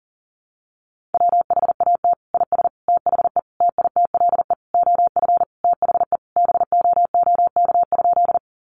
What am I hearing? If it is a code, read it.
Morse code, heard as a keyed tone, about 40 wpm, 700 Hz. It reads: WHAT IS THE TITLE OF THE BOOK?
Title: Sentences from Top 1200 Words 40wpm